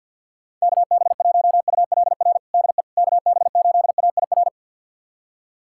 Morse code, heard as a keyed tone, about 53 wpm, 700 Hz. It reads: KB1ULR DE KB8AIR
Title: USA Callsign DE Callsign 50wpm